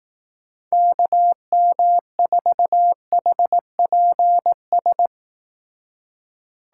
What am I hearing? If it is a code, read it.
Morse code, heard as a keyed tone, about 18 wpm, 700 Hz. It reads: KM4HPS
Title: Speed Racing - USA 2x3 Call Signs 15wpm